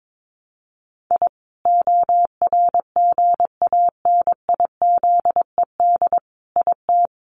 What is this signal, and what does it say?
Morse code, heard as a keyed tone, about 22 wpm, 700 Hz. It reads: I ORGANIZED IT
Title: Sentences from Top 1300 Words 22wpm